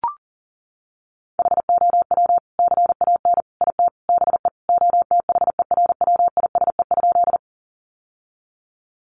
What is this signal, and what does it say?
Morse code, heard as a keyed tone, about 40 wpm, 700 Hz. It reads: HOW CAN IT BE OTHERWISE?
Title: Sentences from Top 1200 Words 40wpm